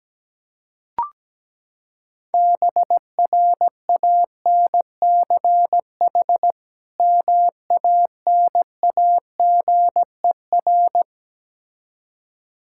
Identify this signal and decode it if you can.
Morse code, heard as a keyed tone, about 17 wpm, 700 Hz. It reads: BRANCH MANAGER
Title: Practice QSO - Exchange 3 - More Info - 17wpm